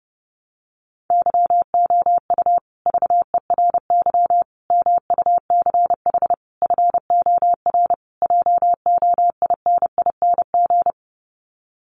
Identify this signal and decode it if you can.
Morse code, heard as a keyed tone, about 30 wpm, 700 Hz. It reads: YOU VERY MUCH FOR JOINING
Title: Sets of 5 Words Encore 30wpm